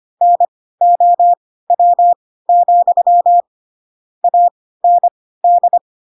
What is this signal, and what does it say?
Morse code, heard as a keyed tone, about 25 wpm, spaced as for 16 wpm, 700 Hz. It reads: NOW, AND
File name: Pinocchio / Pinocchio ch4 10 25 ik1hge